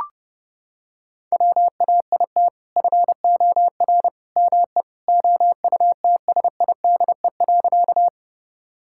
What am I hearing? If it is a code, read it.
Morse code, heard as a keyed tone, about 32 wpm, 700 Hz. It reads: WAIT FOR ME OUTSIDE.
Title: Sentences from Top 1000 Words - Review Part 6 - 30wpm